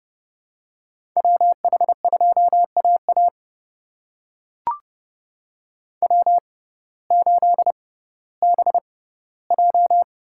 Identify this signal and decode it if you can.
Morse code, heard as a keyed tone, about 30 wpm, spaced as for 17 wpm, 700 Hz. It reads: WH2AA W8BJ
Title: USA Call Signs - 6x Character Spacing - 30wpm